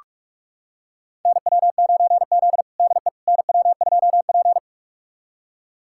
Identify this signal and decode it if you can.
Morse code, heard as a keyed tone, about 44 wpm, 690 Hz. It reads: NW9Z DE NW2P
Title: USA Callsign DE Callsign - 4 Characters - 45wpm